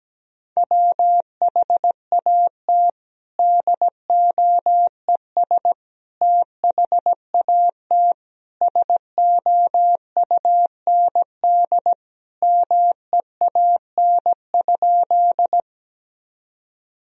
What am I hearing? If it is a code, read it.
Morse code, heard as a keyed tone, about 17 wpm, 700 Hz. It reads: WHAT DOES THAT SOUND MEAN?
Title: Sentences from Top 1000 Words - Review Part 1 - 17wpm